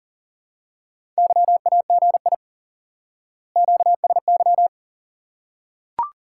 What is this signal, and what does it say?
Morse code, heard as a keyed tone, about 40 wpm, 700 Hz. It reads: YAGI QSY
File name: IQE Course 3 - Lesson 89 - YAGI - Two Elements - 40wpm